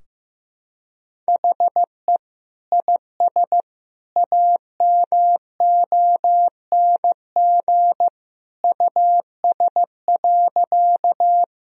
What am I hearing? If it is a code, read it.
Morse code, heard as a keyed tone, about 15 wpm, 700 Hz. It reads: HE IS AMONG US.
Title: Sentences from Top 1000 Words - Review Part 3 - 15wpm